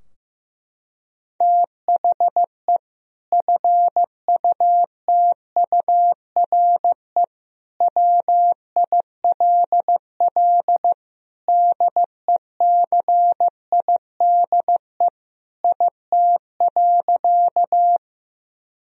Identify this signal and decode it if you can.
Morse code, heard as a keyed tone, about 15 wpm, 700 Hz. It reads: THE FUTURE WILL DECIDE IT.
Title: Sentences from Top 1000 Words - Review Part 3 - 15wpm